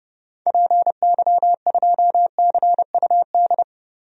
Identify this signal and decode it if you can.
Morse code, heard as a keyed tone, about 30 wpm, 700 Hz. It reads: PY2CUD
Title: DXCC-100 Call Signs - 6 Characters - 30wpm